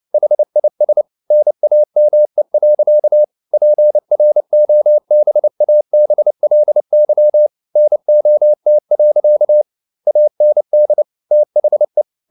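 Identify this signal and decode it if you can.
Morse code, heard as a keyed tone, about 29 wpm, 600 Hz. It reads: HIS NAME. PROBABLY NOT. AND THE
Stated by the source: Morse code at 29 wpm, sending Ken Follett's novel Never in English, from CW med Gnister.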